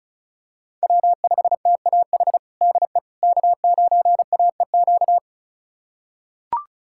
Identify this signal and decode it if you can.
Morse code, heard as a keyed tone, about 36 wpm, 700 Hz. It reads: W5TAH DE K9AEQ
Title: USA Callsign DE Callsign - 5 Characters - 35wpm